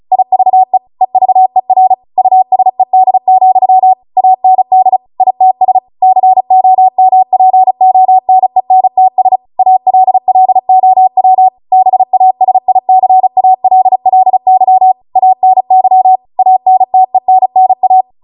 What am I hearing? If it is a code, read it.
Morse code, heard as a keyed tone, about 36 wpm, 750 Hz. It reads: IVE EVER USED, AND ITS COMPONENTS ALLOW BASICALLY ANY ANTENNA